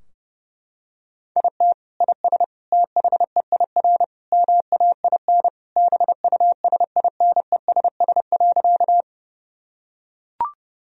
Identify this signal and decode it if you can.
Morse code, heard as a keyed tone, about 30 wpm, 700 Hz. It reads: IT IS THEIR MAIN BUSINESS.
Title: Sentences from Top 1000 Words - Review Part 2 - 30wpm